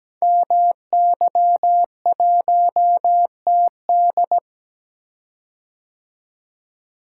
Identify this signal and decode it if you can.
Morse code, heard as a keyed tone, about 17 wpm, 700 Hz. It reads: MY1TD
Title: Code Groups - 5 Letter:Numbers - 17wpm